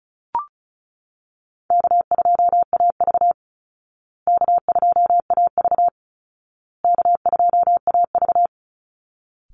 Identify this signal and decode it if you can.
Morse code, heard as a keyed tone, about 34 wpm, 700 Hz. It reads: K2AV K2AV K2AV
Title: Easy Call Signs Repeated 3x - 35wpm